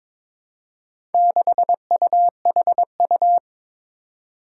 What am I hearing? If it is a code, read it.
Morse code, heard as a keyed tone, about 23 wpm, 700 Hz. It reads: 6UHU
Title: Four Challenging Letter-Number Combinations 22wpm